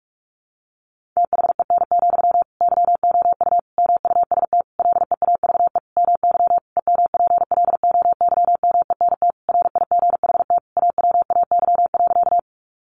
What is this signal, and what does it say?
Morse code, heard as a keyed tone, about 44 wpm, 700 Hz. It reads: THEN, YOU MUST LEAVE MY EMPLOYMENT RIGHT AWAY.
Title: Sentences from Top 1500 Words 45wpm